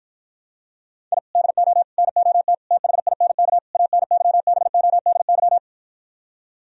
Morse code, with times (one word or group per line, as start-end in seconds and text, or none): 1.1-1.2 I
1.4-1.8 DO
2.0-2.5 NOT
2.7-3.6 THINK
3.7-5.6 ANYBODY